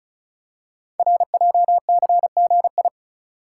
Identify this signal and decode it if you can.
Morse code, heard as a keyed tone, about 36 wpm, 700 Hz. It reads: RJCGI
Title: Code Groups - 5 Letters - 35wpm